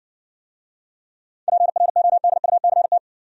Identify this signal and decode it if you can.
Morse code, heard as a keyed tone, about 60 wpm, 700 Hz. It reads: PRODUCT